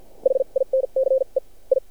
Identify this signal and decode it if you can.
Morse code, heard as a keyed tone, about 50 wpm, spaced as for 39 wpm, 530 Hz. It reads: HINCE I
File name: archived / CRSnet